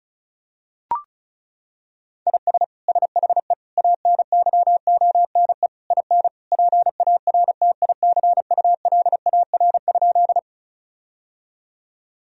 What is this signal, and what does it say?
Morse code, heard as a keyed tone, about 36 wpm, 700 Hz. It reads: IS SHE ANYONE IN PARTICULAR?
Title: Sentences from Top 1000 Words - Review Part 4 - 35wpm